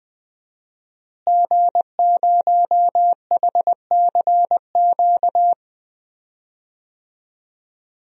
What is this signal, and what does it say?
Morse code, heard as a keyed tone, about 20 wpm, 700 Hz. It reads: G0HCQ
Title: DXCC-100 Call Signs - 5 Characters - Repeated 3x - 20wpm